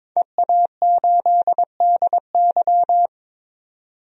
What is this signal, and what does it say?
Morse code, heard as a keyed tone, about 22 wpm, 700 Hz. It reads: EA8DY
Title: DXCC-100 Call Signs - 5 Characters - 22wpm